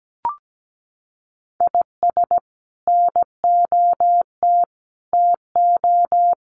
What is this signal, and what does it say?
Morse code, heard as a keyed tone, about 17 wpm, 700 Hz. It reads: IS NOT TO
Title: Sets of 5 Words 17wpm